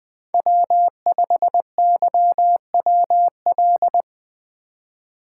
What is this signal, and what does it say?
Morse code, heard as a keyed tone, about 25 wpm, 700 Hz. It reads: W5YWL